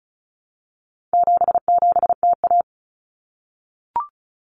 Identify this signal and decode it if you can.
Morse code, heard as a keyed tone, about 34 wpm, 700 Hz. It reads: 77TA
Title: Common Cut Numbers - 4-Digit 35wpm